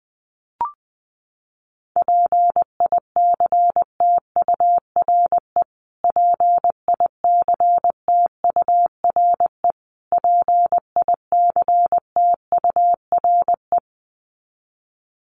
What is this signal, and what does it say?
Morse code, heard as a keyed tone, about 20 wpm, 700 Hz. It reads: PICTURE PICTURE PICTURE
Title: Top 300 Words Repeated 20wpm